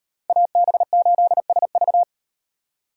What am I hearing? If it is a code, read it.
Morse code, heard as a keyed tone, about 35 wpm, 700 Hz. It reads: AB8SV